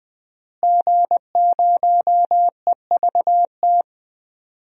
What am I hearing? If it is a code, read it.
Morse code, heard as a keyed tone, about 20 wpm, 700 Hz. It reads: G0EVT
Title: DXCC-100 Call Signs - 5 Characters - Repeated 3x - 20wpm